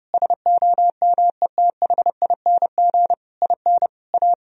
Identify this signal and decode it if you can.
Morse code, heard as a keyed tone, about 30 wpm, 700 Hz. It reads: SOMETHING IN A
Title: Sentences from Top 1000 Words - Review Part 1 - 30wpm